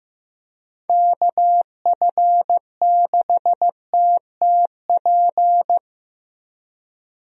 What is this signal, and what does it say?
Morse code, heard as a keyed tone, about 15 wpm, 700 Hz. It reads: KF6TTP
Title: USA Call Signs - 6 Characters - Repeated 3x - 15wpm